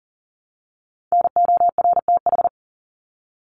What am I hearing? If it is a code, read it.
Morse code, heard as a keyed tone, about 38 wpm, 700 Hz. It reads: NORTH
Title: Top 300 Words 40wpm